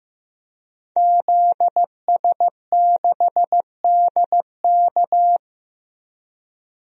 Morse code, heard as a keyed tone, about 15 wpm, 700 Hz. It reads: ZS6DK
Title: DXCC-100 Call Signs - 5 Characters - Repeated 2x - 15wpm